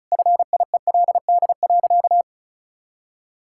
Morse code, heard as a keyed tone, about 36 wpm, 700 Hz. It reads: FIELD.
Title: Sentences from Top 1000 Words - Review Part 6 - 35wpm